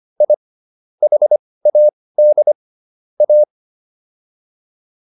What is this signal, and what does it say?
Morse code, heard as a keyed tone, about 25 wpm, spaced as for 18 wpm, 600 Hz. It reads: I HAD A